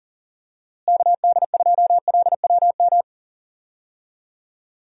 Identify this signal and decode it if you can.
Morse code, heard as a keyed tone, about 39 wpm, 700 Hz. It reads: KD2LWM
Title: USA Call Signs - 6 Characters - Repeated 3x - 40wpm